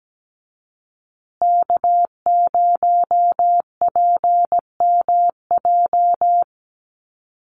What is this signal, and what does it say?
Morse code, heard as a keyed tone, about 17 wpm, 700 Hz. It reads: K0PMJ
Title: USA Call Signs Repeated 3x 17wpm